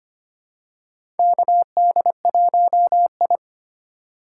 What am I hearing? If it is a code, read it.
Morse code, heard as a keyed tone, about 25 wpm, 700 Hz. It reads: KD1I
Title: USA Call Signs - 3x Character Spacing - 25wpm